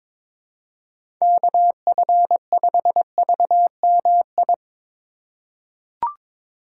Morse code, heard as a keyed tone, about 21 wpm, 700 Hz. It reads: KF5VMI
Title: USA Call Signs - 6 Characters - 2x Character Spacing - 22wpm